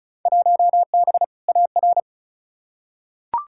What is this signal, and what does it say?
Morse code, heard as a keyed tone, about 36 wpm, 700 Hz. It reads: 1B AR
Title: Field Day Exchanges Encore - 35wpm